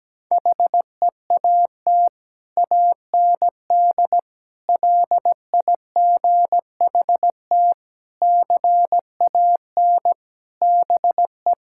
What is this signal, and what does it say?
Morse code, heard as a keyed tone, about 17 wpm, 700 Hz. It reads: HEAT AND LIGHT CAN BE
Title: Sentences from Top 1000 Words - Review Part 6 - 17wpm